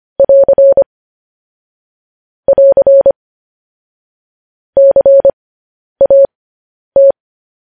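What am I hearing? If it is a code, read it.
Morse code, heard as a keyed tone, about 25 wpm, spaced as for 10 wpm, 550 Hz. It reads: <AR> <AR> CAT
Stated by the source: Morse code practice sending news headlines at 10 wpm.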